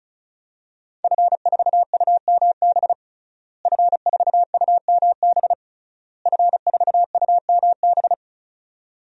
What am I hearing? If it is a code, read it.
Morse code, heard as a keyed tone, about 34 wpm, 700 Hz. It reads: F4UMB F4UMB F4UMB
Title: DXCC-100 Call Signs - 5 Characters - Repeated 3x - 35wpm